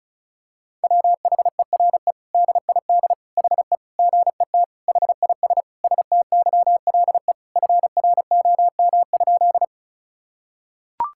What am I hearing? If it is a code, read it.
Morse code, heard as a keyed tone, about 36 wpm, 700 Hz. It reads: WHERE DID HE GET HIS STYLE FROM?
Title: Sentences from Top 1000 Words - Review Part 5 - 35wpm